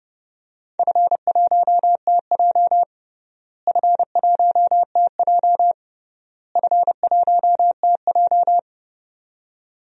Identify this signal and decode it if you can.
Morse code, heard as a keyed tone, about 28 wpm, 700 Hz. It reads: F1TJ F1TJ F1TJ
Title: DXCC-100 Call Signs - 4 Characters - Repeated 3x - 30wpm